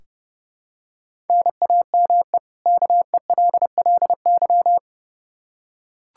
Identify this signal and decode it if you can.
Morse code, heard as a keyed tone, about 31 wpm, 700 Hz. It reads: NAME KELLY
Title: Practice QSO - Exchange 1 - Beginning - 30wpm